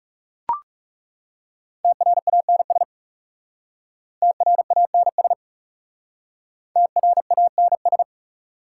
Morse code, heard as a keyed tone, about 40 wpm, 690 Hz. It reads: TRANS TRANS TRANS